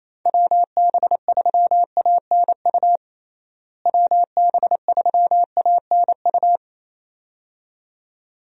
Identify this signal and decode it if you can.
Morse code, heard as a keyed tone, about 28 wpm, 700 Hz. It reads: WB3ANU WB3ANU
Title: USA Call Signs - 6 Characters - Repeated 2x - 28wpm